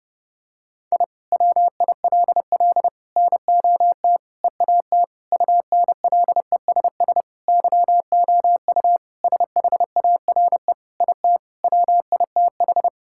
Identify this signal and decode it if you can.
Morse code, heard as a keyed tone, about 30 wpm, 700 Hz. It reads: I WILL NOT EAT UNLESS YOU SHARE IT WITH
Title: Sentences from Top 1000 Words - Review Part 6 - 30wpm